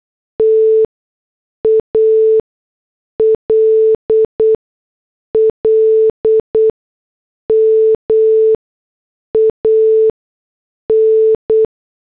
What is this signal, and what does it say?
Morse code, heard as a keyed tone, about 8 wpm, 440 Hz. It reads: T A L L M A N